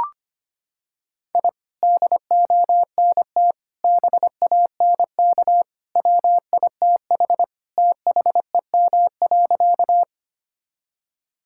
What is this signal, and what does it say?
Morse code, heard as a keyed tone, about 25 wpm, 700 Hz. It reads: I DONT BANK WITH THEM.
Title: Sentences from Top 1000 Words - Review Part 6 - 25wpm